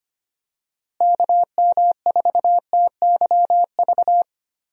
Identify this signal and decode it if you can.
Morse code, heard as a keyed tone, about 25 wpm, 700 Hz. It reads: KM4TYV
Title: USA Call Signs - 6 Characters - Repeated 3x - 25wpm